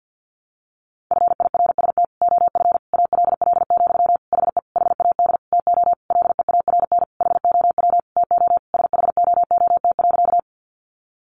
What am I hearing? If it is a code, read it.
Morse code, heard as a keyed tone, about 50 wpm, 700 Hz. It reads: FIRST OF ALL, HE HAD TO LEARN HOW TO SHOOT.
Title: Sentences from Top 1200 Words 50wpm